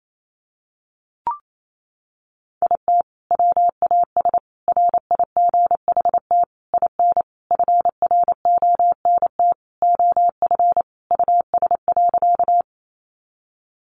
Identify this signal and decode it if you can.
Morse code, heard as a keyed tone, about 28 wpm, 700 Hz. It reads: IT WAS RIGHT IN FRONT OF US.
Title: Sentences from Top 500 Words 28wpm